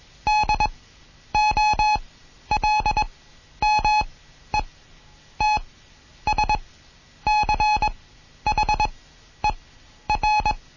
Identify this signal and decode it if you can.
Morse code, heard as a keyed tone, about 21 wpm, spaced as for 10 wpm, 850 Hz. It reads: DOLMETSCHER
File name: Morsen / Kurs A 20